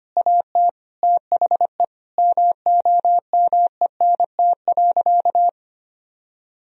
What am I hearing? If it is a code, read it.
Morse code, heard as a keyed tone, about 25 wpm, 700 Hz. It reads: AT THE MOMENT.
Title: Sentences from Top 1000 Words - Review Part 3 - 25wpm